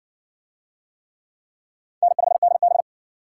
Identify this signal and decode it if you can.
Morse code, heard as a keyed tone, about 60 wpm, 700 Hz. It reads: N5DB